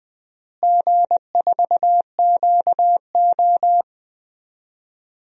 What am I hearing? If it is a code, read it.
Morse code, heard as a keyed tone, about 21 wpm, 690 Hz. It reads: G4QO
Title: DXCC-100 Call Signs - 4 Characters - Repeated 3x - 20wpm